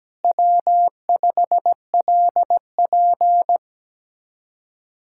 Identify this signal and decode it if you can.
Morse code, heard as a keyed tone, about 17 wpm, 700 Hz. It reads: W5LP
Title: Speed-Racing Common QSO Phrases - 17wpm